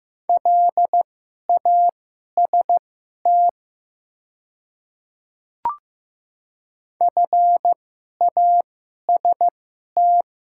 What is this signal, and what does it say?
Morse code, heard as a keyed tone, about 15 wpm, spaced as for 11 wpm, 700 Hz. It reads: LAST FAST
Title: 2x Character Spacing - Top 500 Words - 15wpm